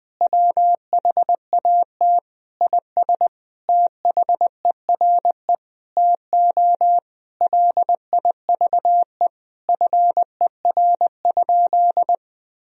Morse code, heard as a keyed tone, about 20 wpm, 700 Hz. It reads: WHAT IS THERE TO LIVE FER?
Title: Sentences from Top 200 Words and Abbreviations 20wpm